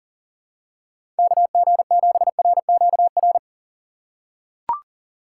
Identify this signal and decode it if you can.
Morse code, heard as a keyed tone, about 40 wpm, 700 Hz. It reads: KG7RQR
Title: USA Call Signs - 6 Characters - 2x Character Spacing - 40wpm